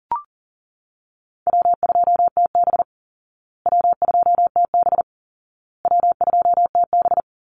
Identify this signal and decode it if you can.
Morse code, heard as a keyed tone, about 40 wpm, 700 Hz. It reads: W2TB W2TB W2TB
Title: Easy Call Signs Repeated 3x - 40wpm